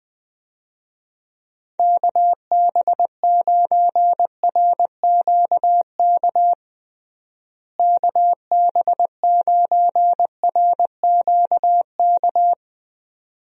Morse code, heard as a keyed tone, about 21 wpm, 700 Hz. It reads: KB9RQK KB9RQK
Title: USA Call Signs - 6 Characters - Repeated 2x - 20wpm